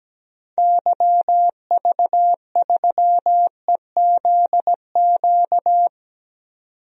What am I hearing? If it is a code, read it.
Morse code, heard as a keyed tone, about 17 wpm, 700 Hz. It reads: YV3EZQ
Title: DXCC-100 Call Signs - 6 Characters - 17wpm